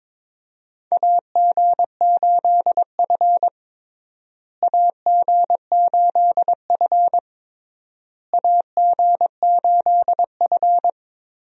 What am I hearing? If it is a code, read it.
Morse code, heard as a keyed tone, about 22 wpm, 700 Hz. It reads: AG8F AG8F AG8F
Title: USA Call Signs - 4 Characters - Repeated 3x - 22wpm